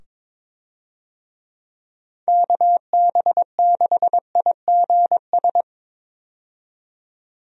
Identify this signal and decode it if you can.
Morse code, heard as a keyed tone, about 22 wpm, 700 Hz. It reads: KB6IGS
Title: USA Call Signs - 6 Characters - 22wpm